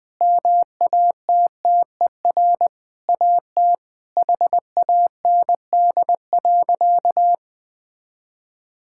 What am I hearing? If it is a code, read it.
Morse code, heard as a keyed tone, about 20 wpm, 700 Hz. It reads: MATTER AT HAND.
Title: Sentences from Top 1600 Words 20wpm